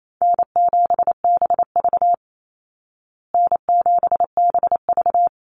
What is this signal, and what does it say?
Morse code, heard as a keyed tone, about 28 wpm, 700 Hz. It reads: N7BV N7BV
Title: Easy Call Signs Repeated 2x - 28wpm